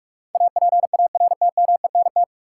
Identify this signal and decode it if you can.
Morse code, heard as a keyed tone, about 41 wpm, 700 Hz. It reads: APARTMENT
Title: Sentences from Top 900 Words - New Word Drill - 45wpm